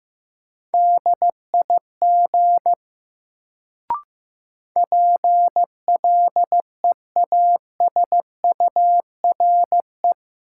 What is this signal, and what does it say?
Morse code, heard as a keyed tone, about 15 wpm, 700 Hz. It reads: DIG PLEASURE
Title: Sentences from Top 1600 Words - New Word Drill Alt - 15wpm